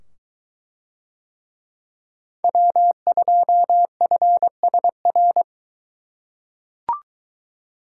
Morse code, heard as a keyed tone, about 22 wpm, 700 Hz. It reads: W2FSR